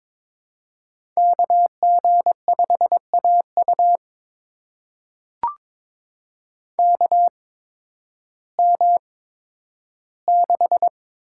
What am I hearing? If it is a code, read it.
Morse code, heard as a keyed tone, about 22 wpm, spaced as for 12 wpm, 700 Hz. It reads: KG5AU KM6
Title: USA Call Signs - 8x Character Spacing - 22wpm